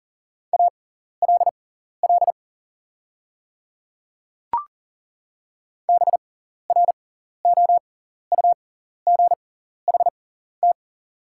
Morse code, heard as a keyed tone, about 40 wpm, spaced as for 14 wpm, 690 Hz. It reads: ALL BROUGHT
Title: 6x Character Spacing - Top 500 Words - 40wpm